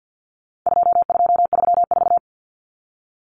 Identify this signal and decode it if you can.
Morse code, heard as a keyed tone, about 50 wpm, 700 Hz. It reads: 2234
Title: Common Cut Numbers - 4-Digit 50wpm